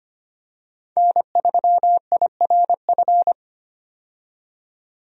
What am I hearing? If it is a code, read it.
Morse code, heard as a keyed tone, about 25 wpm, 700 Hz. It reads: N3IRF
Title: USA Call Signs - 5 Characters - 25wpm